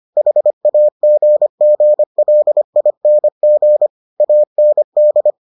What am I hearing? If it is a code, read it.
Morse code, heard as a keyed tone, about 25 wpm, 600 Hz. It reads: HAGGLING AND